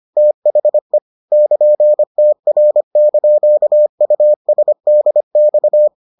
Morse code, heard as a keyed tone, about 25 wpm, 600 Hz. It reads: THE <KN>TR)USDX